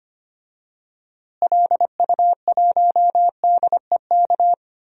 Morse code, heard as a keyed tone, about 25 wpm, 700 Hz. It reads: LU1DEK
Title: DXCC-100 Call Signs - 6 Characters - Repeated 3x - 25wpm